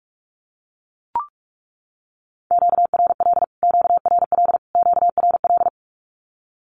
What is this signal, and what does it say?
Morse code, heard as a keyed tone, about 44 wpm, 700 Hz. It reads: QRL QRL QRL
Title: Q-Codes Repeated 3x 45wpm